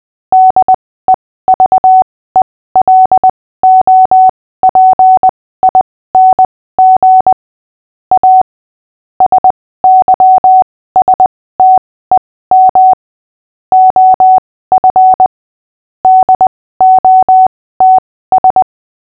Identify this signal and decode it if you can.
Morse code, heard as a keyed tone, about 20 wpm, spaced as for 15 wpm, 750 Hz. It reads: DEVELOPING A SYSTEM OF DOTS
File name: cw / fists / Short-Story-6-15wpm